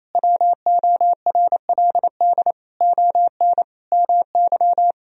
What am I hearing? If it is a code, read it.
Morse code, heard as a keyed tone, about 28 wpm, 700 Hz. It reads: WORLD ON MY
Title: Sentences from Top 1000 Words - Review Part 1 - 28wpm